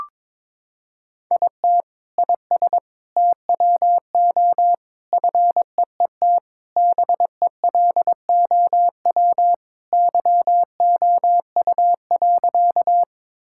Morse code, heard as a keyed tone, about 22 wpm, 700 Hz. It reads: IT IS TWO FEET BELOW YOU.
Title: Sentences from Top 1000 Words - Review Part 5 - 22wpm